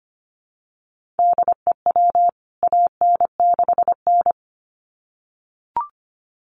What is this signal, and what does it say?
Morse code, heard as a keyed tone, about 25 wpm, 700 Hz. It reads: DEW AN6N
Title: CWT Exchange 25wpm